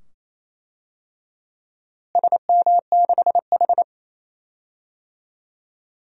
Morse code, heard as a keyed tone, about 28 wpm, 700 Hz. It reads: SM6H
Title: DXCC-100 Call Signs - 4 Characters - Repeated 2x - 28wpm